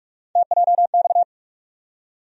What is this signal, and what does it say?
Morse code, heard as a keyed tone, about 42 wpm, 700 Hz. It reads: TJX
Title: USA Call Sign Postfixes - 3 Letters 45wpm